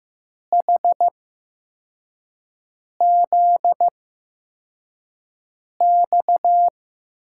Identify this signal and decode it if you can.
Morse code, heard as a keyed tone, about 15 wpm, spaced as for 4 wpm, 700 Hz. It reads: HZX
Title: USA Call Signs - 6 Characters - 8x Character Spacing - 15wpm